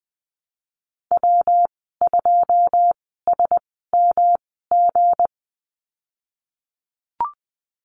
Morse code, heard as a keyed tone, about 20 wpm, spaced as for 15 wpm, 700 Hz. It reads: W2SMG
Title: 186 Lesson 30g - Letter b - Call Signs 20wpm